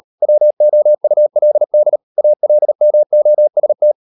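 Morse code, heard as a keyed tone, about 38 wpm, 600 Hz. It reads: WOULD ALMOST
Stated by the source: Morse code practice at 35–39 wpm sending English text of Ken Follett's novel Never, from CW med Gnister.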